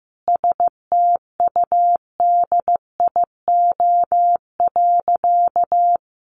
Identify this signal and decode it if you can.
Morse code, heard as a keyed tone, about 15 wpm, 700 Hz. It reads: STUDIO.
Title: Sentences from Top 1200 Words 15wpm